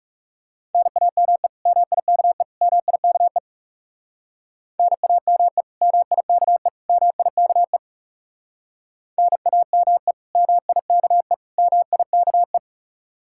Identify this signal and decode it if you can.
Morse code, heard as a keyed tone, about 37 wpm, 700 Hz. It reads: NAME MIKE MIKE NAME MIKE MIKE NAME MIKE MIKE
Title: Speed-Racing Common QSO Phrases - 30wpm